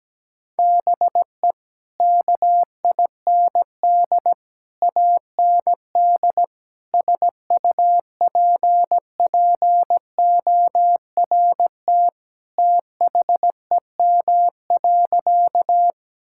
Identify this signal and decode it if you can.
Morse code, heard as a keyed tone, about 17 wpm, 700 Hz. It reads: BE KIND AND SUPPORT THEM.
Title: Sentences from Top 1000 Words - Review Part 5 - 17wpm